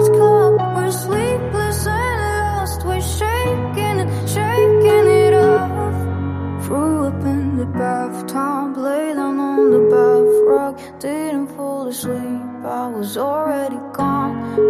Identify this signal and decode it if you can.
Ringback tone (Europe, 425 Hz), with noise.